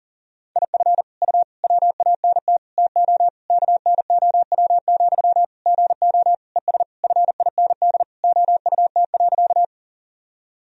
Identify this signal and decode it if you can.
Morse code, heard as a keyed tone, about 40 wpm, 700 Hz. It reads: IF U WANT TO KNOW, GO ES FIND OUT.
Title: Sentences from Top 200 Words and Abbreviations 40wpm